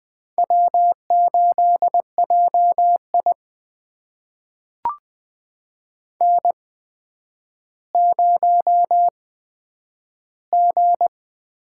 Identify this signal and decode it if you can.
Morse code, heard as a keyed tone, about 20 wpm, spaced as for 10 wpm, 700 Hz. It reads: W8JI N0G
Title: USA Call Signs - 8x Character Spacing - 20wpm